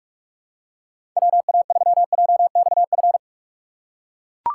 Morse code, heard as a keyed tone, about 42 wpm, 700 Hz. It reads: WA3JXF
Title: USA Call Signs - 6 Characters - 3x Character Spacing - 45wpm